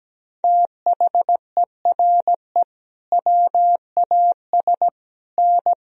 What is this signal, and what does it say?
Morse code, heard as a keyed tone, about 17 wpm, 700 Hz. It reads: THERE WAS N
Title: Sentences from Top 300 Words and Abbreviations 17wpm